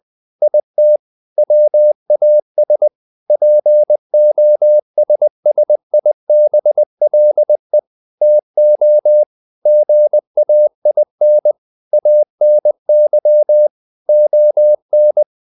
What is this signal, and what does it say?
Morse code, heard as a keyed tone, about 20 wpm, 600 Hz. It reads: IT WAS POSSIBLE TO GAIN ANY ON